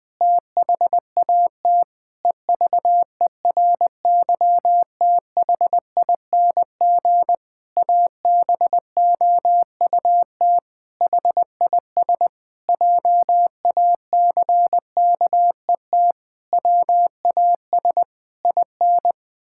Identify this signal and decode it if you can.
Morse code, heard as a keyed tone, about 21 wpm, 700 Hz. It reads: THAT EVERYTHING ABOUT HIS JACKET WAS IN
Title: Pangram Challenge v2 - 20wpm